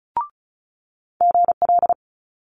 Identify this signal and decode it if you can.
Morse code, heard as a keyed tone, about 36 wpm, 700 Hz. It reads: GL
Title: Multi-letter Phonemes Quick Fire 35wpm